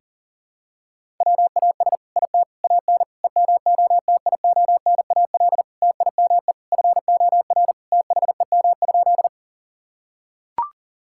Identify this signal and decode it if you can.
Morse code, heard as a keyed tone, about 37 wpm, 700 Hz. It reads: WAS IT AN EMOTIONAL TIME FOR THEM?
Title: Sentences from Top 1000 Words - Review Part 5 - 40wpm